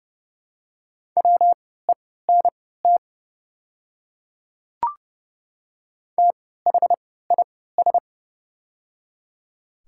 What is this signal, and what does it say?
Morse code, heard as a keyed tone, about 29 wpm, spaced as for 17 wpm, 700 Hz. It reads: WENT THIS
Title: 3x Character Spacing - Top 200 Words - 30wpm